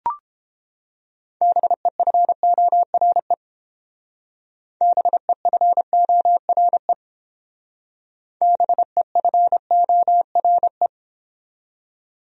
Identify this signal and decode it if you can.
Morse code, heard as a keyed tone, about 29 wpm, 700 Hz. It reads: BEFORE BEFORE BEFORE